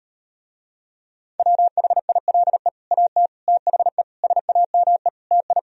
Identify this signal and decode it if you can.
Morse code, heard as a keyed tone, about 38 wpm, 700 Hz. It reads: WHILE AT THE SAME TI